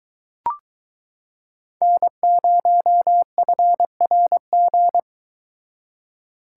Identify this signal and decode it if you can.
Morse code, heard as a keyed tone, about 22 wpm, 700 Hz. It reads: N0FRG